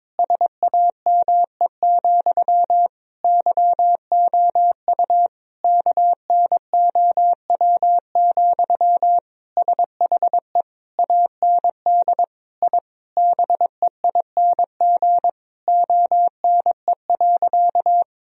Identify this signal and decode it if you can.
Morse code, heard as a keyed tone, about 22 wpm, 700 Hz. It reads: SAME, YOU KNOW, SHE AND I BEING ONE.
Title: Sentences from Top 1000 Words - Review Part 3 - 22wpm